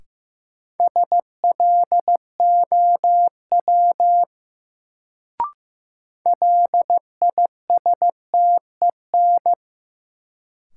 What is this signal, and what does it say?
Morse code, heard as a keyed tone, about 15 wpm, 700 Hz. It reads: SLOW LISTEN
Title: Sentences from Top 400 Words - New Word Drill - 15wpm